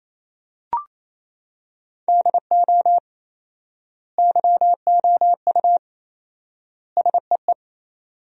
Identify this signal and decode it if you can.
Morse code, heard as a keyed tone, about 28 wpm, 700 Hz. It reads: DO YOU SEE
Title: 4x Word-Spacing - Sentences from Top 100 Words - 28wpm